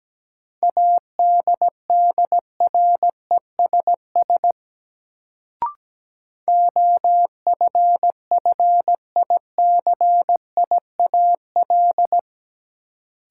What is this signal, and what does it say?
Morse code, heard as a keyed tone, about 17 wpm, 700 Hz. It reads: ADDRESS OFFICIAL
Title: Sentences from Top 1000 Words - New Word Drill - 17wpm